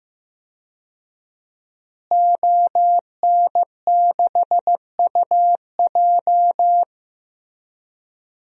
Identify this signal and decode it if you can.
Morse code, heard as a keyed tone, about 15 wpm, 700 Hz. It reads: ON6UJ
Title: DXCC-100 Call Signs - 5 Characters - Repeated 3x - 15wpm